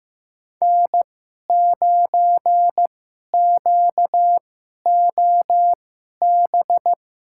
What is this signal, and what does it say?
Morse code, heard as a keyed tone, about 15 wpm, spaced as for 11 wpm, 700 Hz. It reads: N9QOB
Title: USA Call Signs - 5 Characters - 2x Character Spacing - 15wpm